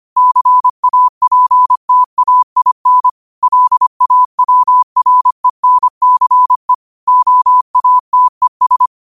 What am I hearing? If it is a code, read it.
Morse code, heard as a keyed tone, about 25 wpm, 1000 Hz. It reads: CAPTAIN LAWRENCE OATES
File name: qotdfiles / 25wpm